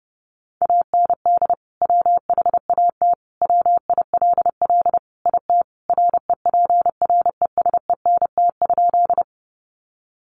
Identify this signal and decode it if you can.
Morse code, heard as a keyed tone, about 30 wpm, 700 Hz. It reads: AND WHAT WILL IT REPRESENT?
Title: Sentences from Top 1200 Words 30wpm